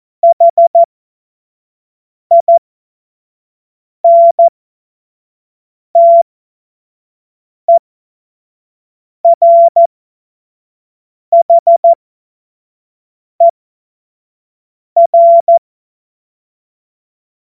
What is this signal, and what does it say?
Morse code, heard as a keyed tone, about 14 wpm, spaced as for 5 wpm, 680 Hz. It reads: HINTERHER
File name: Morsekurs / Witze / Witz 11. Januar 2022, 25 F70